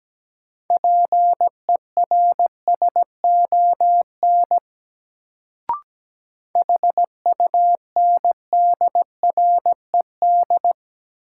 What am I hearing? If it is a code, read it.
Morse code, heard as a keyed tone, about 17 wpm, 700 Hz. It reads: PERSON HUNDRED
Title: Sentences from Top 400 Words - New Word Drill - 17wpm